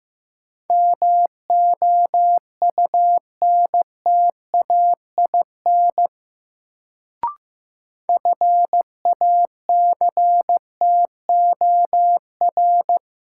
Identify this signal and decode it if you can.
Morse code, heard as a keyed tone, about 15 wpm, 700 Hz. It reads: MOUNTAIN FACTOR
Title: Sentences from Top 1300 Words - New Word Drill Alt - 15wpm